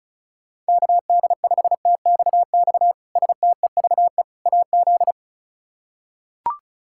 Morse code, heard as a keyed tone, about 36 wpm, 700 Hz. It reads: KD5TXX STEVE AZ
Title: K1USN SST QSO Practice - 35wpm